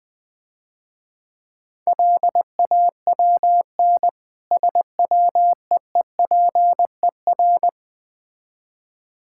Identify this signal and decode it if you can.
Morse code, heard as a keyed tone, about 20 wpm, 690 Hz. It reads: LAWN SWEEPER
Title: Difficult Words - Speed-Racing with Hints - 20wpm